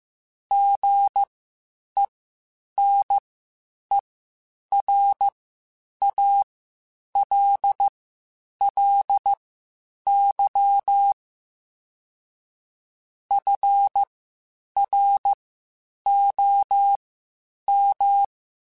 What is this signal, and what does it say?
Morse code, heard as a keyed tone, about 15 wpm, spaced as for 8 wpm, 800 Hz. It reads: GENERALLY FROM
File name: Spaced Characters / Spaced Characters-mp3s / Weather 4 14 sp @13